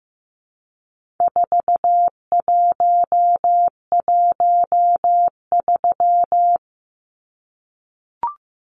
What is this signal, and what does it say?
Morse code, heard as a keyed tone, about 15 wpm, 700 Hz. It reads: 4113
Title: Numbers - 4-Digit 15wpm